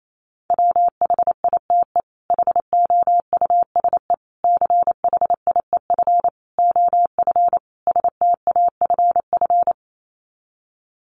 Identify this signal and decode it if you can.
Morse code, heard as a keyed tone, about 28 wpm, 700 Hz. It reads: WHITE HOUSE CHIEF OF STAFF
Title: Sets of 5 Words Encore 28wpm